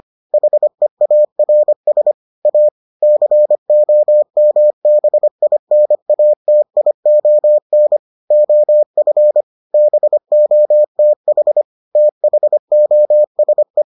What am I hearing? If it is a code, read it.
Morse code, heard as a keyed tone, about 25 wpm, 600 Hz. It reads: HEARS A COMBINATION OF BOTH THOSE